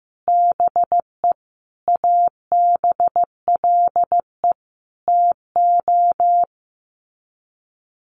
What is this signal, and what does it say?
Morse code, heard as a keyed tone, about 15 wpm, 700 Hz. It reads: BE ABLE TO
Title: Sets of 5 Words 15wpm